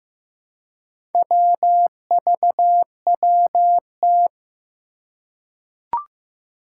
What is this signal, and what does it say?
Morse code, heard as a keyed tone, about 15 wpm, 700 Hz. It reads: WVWT
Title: Four Challenging Letter-Number Combinations 15wpm